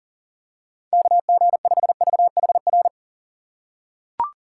Morse code, heard as a keyed tone, about 40 wpm, 700 Hz. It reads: KG5VHR
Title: USA Call Signs - 6 Characters - 2x Character Spacing - 40wpm